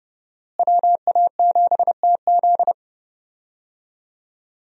Morse code, heard as a keyed tone, about 29 wpm, 700 Hz. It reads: WA7TZ
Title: USA Call Signs - 5 Characters - Repeated 2x - 30wpm